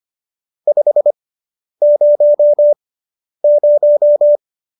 Morse code, heard as a keyed tone, about 25 wpm, spaced as for 10 wpm, 590 Hz. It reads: 500